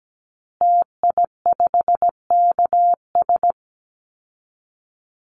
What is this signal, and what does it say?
Morse code, heard as a keyed tone, about 17 wpm, 700 Hz. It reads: TI5KS
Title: Call Signs Repeated 2x - 17wpm